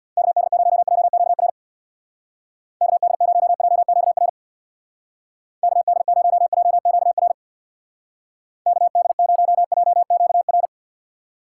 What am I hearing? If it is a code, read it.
Morse code, heard as a keyed tone, about 62 wpm, 700 Hz. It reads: KD0JQR KD0JQR KD0JQR KD0JQR
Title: Speed Racing - USA 2x3 Call Signs 50wpm